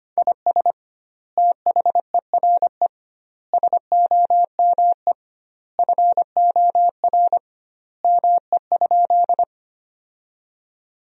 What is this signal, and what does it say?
Morse code, heard as a keyed tone, about 25 wpm, 700 Hz. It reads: IS THERE SOME FOR ME?
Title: 2x Word-Spacing - Sentences from Top 200 Words - 25wpm